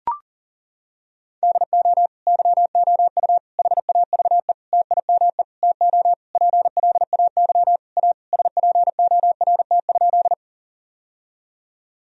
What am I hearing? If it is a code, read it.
Morse code, heard as a keyed tone, about 40 wpm, 700 Hz. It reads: DO YOU HAVE TIME TO PLAY A SPORT?
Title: Sentences from Top 1000 Words - Review Part 4 - 40wpm